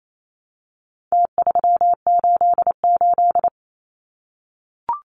Morse code, heard as a keyed tone, about 28 wpm, 700 Hz. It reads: T388
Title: Numbers - Contest Style - 4-Digit 28wpm